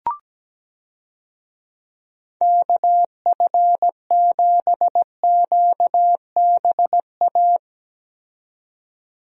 Morse code, heard as a keyed tone, about 17 wpm, 700 Hz. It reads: KF7QBA
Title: USA Call Signs - 6 Characters - 17wpm